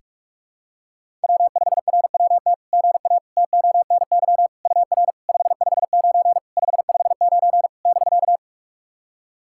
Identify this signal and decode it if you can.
Morse code, heard as a keyed tone, about 44 wpm, 690 Hz. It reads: W5RWT GA TONY UR 559 559 <BK>
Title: SOTA QSO Practice 45wpm